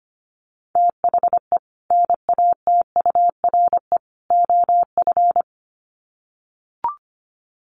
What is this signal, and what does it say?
Morse code, heard as a keyed tone, about 25 wpm, 700 Hz. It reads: THE NATURE OF
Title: Sets of 3 Words 25wpm